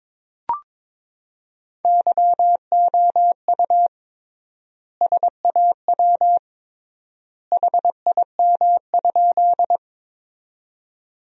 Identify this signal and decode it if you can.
Morse code, heard as a keyed tone, about 21 wpm, 700 Hz. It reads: YOU SAW HIM?
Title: 3x Word-Spacing - Sentences from Top 300 Words - 22wpm